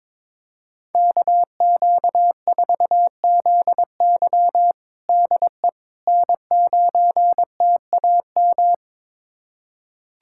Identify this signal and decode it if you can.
Morse code, heard as a keyed tone, about 21 wpm, 700 Hz. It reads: KQ4ZY DE N9TAM
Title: USA Callsign DE Callsign - 5 Characters - 22wpm